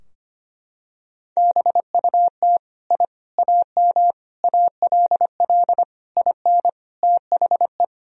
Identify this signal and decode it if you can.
Morse code, heard as a keyed tone, about 25 wpm, 700 Hz. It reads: BUT I AM ALL IN THE
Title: Sentences from Top 1000 Words - Review Part 2 - 25wpm